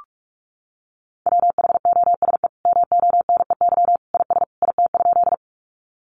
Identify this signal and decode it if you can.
Morse code, heard as a keyed tone, about 44 wpm, 700 Hz. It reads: WHOSE MONEY IS IT?
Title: Sentences from Top 1200 Words 45wpm